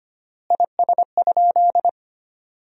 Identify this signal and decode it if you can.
Morse code, heard as a keyed tone, about 25 wpm, 700 Hz. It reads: IS?
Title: 3x Word-Spacing - Sentences from Top 300 Words - 25wpm